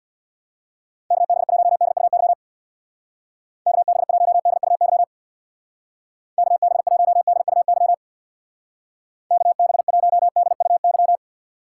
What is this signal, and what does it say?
Morse code, heard as a keyed tone, about 61 wpm, 700 Hz. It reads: KB1DUY KB1DUY KB1DUY KB1DUY
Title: Speed Racing - USA 2x3 Call Signs 50wpm